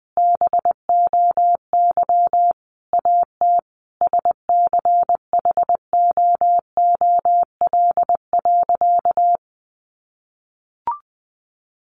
Morse code, heard as a keyed tone, about 20 wpm, 700 Hz. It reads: BOY AT SCHOOL.
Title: Sentences from Top 300 Words 20wpm